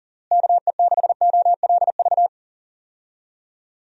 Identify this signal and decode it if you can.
Morse code, heard as a keyed tone, about 40 wpm, 700 Hz. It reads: KE6OLV
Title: USA Call Signs - 6 Characters - Repeated 3x - 40wpm